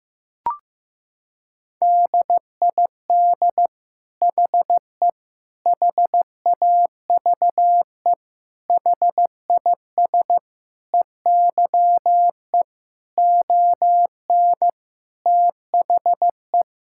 Morse code, heard as a keyed tone, about 16 wpm, 700 Hz. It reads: DID HE HAVE HIS EYE ON THE
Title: Sentences from Top 1000 Words - Review Part 5 - 15wpm